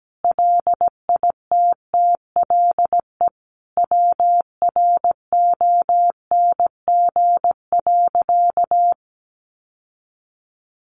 Morse code, heard as a keyed tone, about 17 wpm, 700 Hz. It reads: LITTLE WRONG.
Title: Sentences from Top 900 Words 17wpm